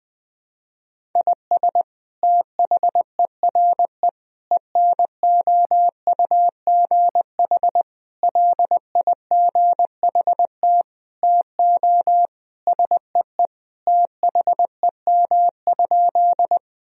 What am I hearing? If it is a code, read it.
Morse code, heard as a keyed tone, about 20 wpm, 700 Hz. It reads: IS THERE ENOUGH LIGHT TO SEE THEM?
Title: Sentences from Top 1000 Words - Review Part 1 - 20wpm